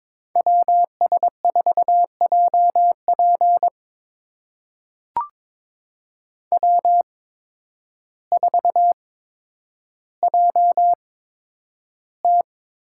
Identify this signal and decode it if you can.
Morse code, heard as a keyed tone, about 22 wpm, spaced as for 10 wpm, 700 Hz. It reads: WS4JP W4JT
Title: USA Call Signs - 5 Characters - 8x Character Spacing - 22wpm